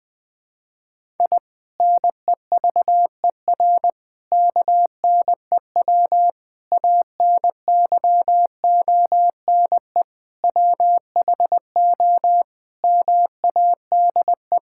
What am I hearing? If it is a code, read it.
Morse code, heard as a keyed tone, about 20 wpm, 700 Hz. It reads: I NEVER KNEW ANYONE WHO MADE
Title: Sentences from Top 1000 Words - Review Part 6 - 20wpm